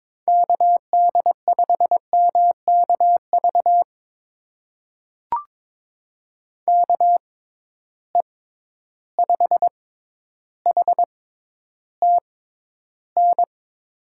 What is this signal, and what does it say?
Morse code, heard as a keyed tone, about 22 wpm, spaced as for 11 wpm, 700 Hz. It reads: KD5MKV KE5HTN
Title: USA Call Signs - 6 Characters - 6x Character Spacing - 22wpm